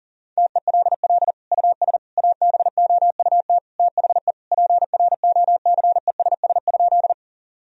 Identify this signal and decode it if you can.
Morse code, heard as a keyed tone, about 40 wpm, 700 Hz. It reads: TELL US ABOUT THE PROCESS?
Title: Sentences from Top 1000 Words - Review Part 3 - 40wpm